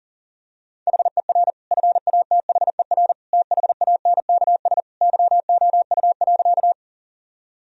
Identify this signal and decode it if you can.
Morse code, heard as a keyed tone, about 40 wpm, 700 Hz. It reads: HER FATHER THANKS YOU.
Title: Sentences from Top 1000 Words - Review Part 3 - 40wpm